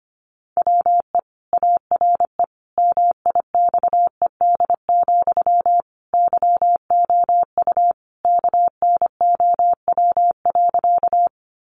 Morse code, heard as a keyed tone, about 25 wpm, 700 Hz. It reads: WE ARE MIXED, YOU KNOW.
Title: Sentences from Top 1500 Words 25wpm